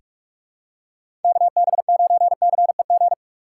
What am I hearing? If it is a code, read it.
Morse code, heard as a keyed tone, about 44 wpm, 700 Hz. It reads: KB9CEG
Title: USA Call Signs - 4x Character Spacing - 45wpm